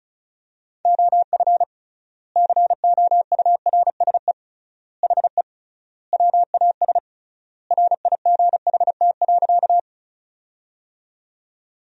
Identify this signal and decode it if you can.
Morse code, heard as a keyed tone, about 34 wpm, 700 Hz. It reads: OF COURSE HE WAS RIGHT.
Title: 3x Word-Spacing - Sentences from Top 500 Words - 35wpm